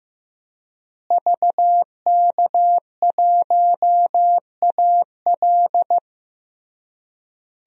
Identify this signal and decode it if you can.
Morse code, heard as a keyed tone, about 15 wpm, 700 Hz. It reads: VK1AL
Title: DXCC-100 Call Signs - 5 Characters - Repeated 3x - 15wpm